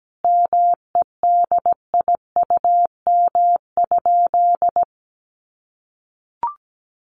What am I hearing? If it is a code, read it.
Morse code, heard as a keyed tone, about 17 wpm, 700 Hz. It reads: MEDIUM?
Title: Sentences from Top 900 Words 17wpm